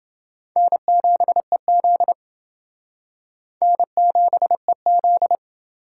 Morse code, heard as a keyed tone, about 28 wpm, 700 Hz. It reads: N7EZ N7EZ